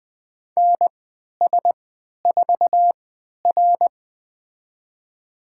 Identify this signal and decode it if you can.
Morse code, heard as a keyed tone, about 20 wpm, spaced as for 11 wpm, 700 Hz. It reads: NS4R
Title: USA Call Signs - 4 Characters - 3x Character Spacing - 20wpm